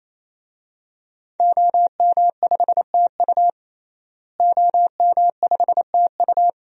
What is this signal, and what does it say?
Morse code, heard as a keyed tone, about 27 wpm, 700 Hz. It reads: OM5TU OM5TU
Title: DXCC-100 Call Signs - 5 Characters - Repeated 3x - 28wpm